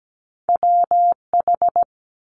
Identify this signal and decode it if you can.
Morse code, heard as a keyed tone, about 17 wpm, 700 Hz. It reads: WH
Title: Multi-letter Phonemes 17wpm